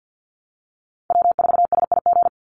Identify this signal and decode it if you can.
Morse code, heard as a keyed tone, about 49 wpm, 700 Hz. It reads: W4SIG
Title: Call Signs - 50wpm